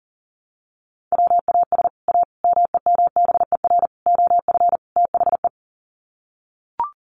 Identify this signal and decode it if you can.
Morse code, heard as a keyed tone, about 40 wpm, 700 Hz. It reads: WAS A MEMBER OF THE
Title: Sets of 5 Words 40wpm